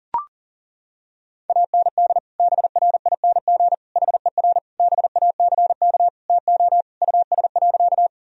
Morse code, heard as a keyed tone, about 37 wpm, 700 Hz. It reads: AND BRING HER BACK TO US.
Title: Sentences from Top 1000 Words - Review Part 3 - 40wpm